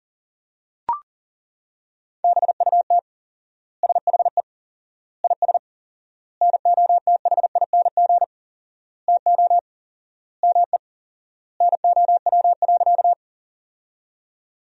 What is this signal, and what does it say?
Morse code, heard as a keyed tone, about 40 wpm, 700 Hz. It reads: BUT SHE IS NOTHING TO ME NOW.
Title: 4x Word-Spacing - Sentences from Top 500 Words - 40wpm